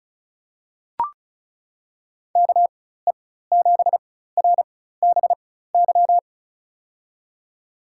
Morse code, heard as a keyed tone, about 34 wpm, spaced as for 16 wpm, 700 Hz. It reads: KE7RBY
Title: USA Call Signs - 4x Character Spacing - 35wpm